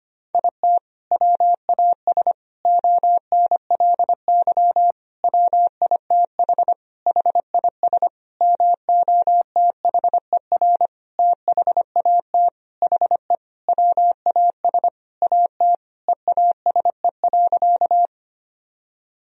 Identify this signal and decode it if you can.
Morse code, heard as a keyed tone, about 25 wpm, 700 Hz. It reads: IT WAS ONLY WITH HIS MOTHER THAT HE WAS AT EASE.
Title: Sentences from Top 1000 Words - Review Part 1 - 25wpm